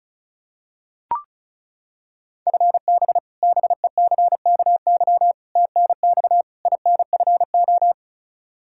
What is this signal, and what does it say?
Morse code, heard as a keyed tone, about 34 wpm, 700 Hz. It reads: FB BECKY TNX INFO
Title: Practice QSO - Exchange 3 - More Info - 35wpm